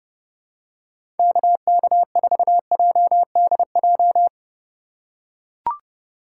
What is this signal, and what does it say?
Morse code, heard as a keyed tone, about 30 wpm, 700 Hz. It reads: KK4JDJ
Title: USA Call Signs - 6 Characters - 3x Character Spacing - 30wpm